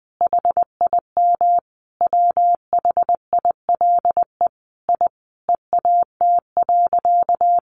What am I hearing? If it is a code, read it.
Morse code, heard as a keyed tone, about 20 wpm, 700 Hz. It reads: HIM WHILE I EAT.
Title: Sentences from Top 1500 Words 20wpm